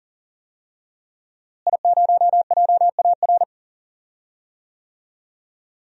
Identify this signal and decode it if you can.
Morse code, heard as a keyed tone, about 40 wpm, 700 Hz. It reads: I0JAR
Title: DXCC-100 Call Signs - 5 Characters - Repeated 3x - 40wpm